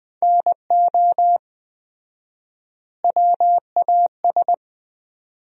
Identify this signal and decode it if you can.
Morse code, heard as a keyed tone, about 20 wpm, 700 Hz. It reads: NO WAS
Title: 4x Word-Spacing - Sentences from Top 100 Words - 20wpm